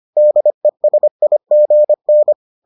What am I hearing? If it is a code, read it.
Morse code, heard as a keyed tone, about 25 wpm, 600 Hz. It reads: DESIGN